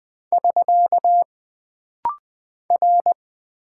Morse code, heard as a keyed tone, about 20 wpm, 700 Hz. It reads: <SK> R
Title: Single Character (Letter:Number:Punctuation:Prosign) - Rapid-Fire - 20wpm